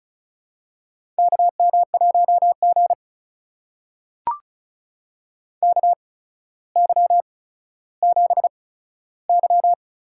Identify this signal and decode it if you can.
Morse code, heard as a keyed tone, about 35 wpm, spaced as for 15 wpm, 700 Hz. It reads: KM1G KY7Y
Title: USA Call Signs - 4 Characters - 8x Character Spacing - 35wpm